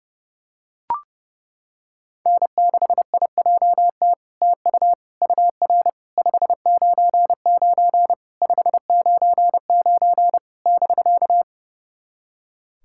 Morse code, heard as a keyed tone, about 32 wpm, 700 Hz. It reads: N6IJT TU UR 599 599 <BK>
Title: SOTA QSO Practice 30wpm